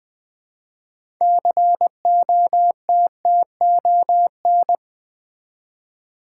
Morse code, heard as a keyed tone, about 20 wpm, 700 Hz. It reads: COTTON
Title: 6 Letter Words - First 3 Letters Given - 20wpm